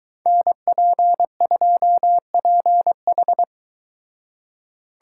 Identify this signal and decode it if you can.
Morse code, heard as a keyed tone, about 22 wpm, 700 Hz. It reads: NP2PH